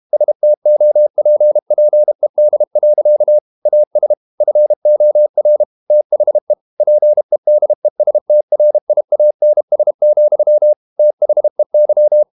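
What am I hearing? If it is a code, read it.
Morse code, heard as a keyed tone, about 32 wpm, 600 Hz. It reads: STOPPED. AS FOR THE PEDESTRIANS, THEY
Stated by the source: Morse code practice at 32 wpm from CW med Gnister, reading Ken Follett's novel Never in English.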